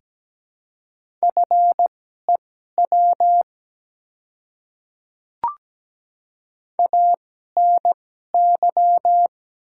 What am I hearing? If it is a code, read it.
Morse code, heard as a keyed tone, about 17 wpm, spaced as for 12 wpm, 690 Hz. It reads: FEW ANY
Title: 2x Character Spacing - Top 300 Words - 17wpm